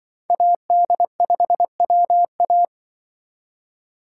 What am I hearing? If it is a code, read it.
Morse code, heard as a keyed tone, about 24 wpm, 700 Hz. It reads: AD5WA
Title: Speed Racing - USA 2x2 Call Signs 20wpm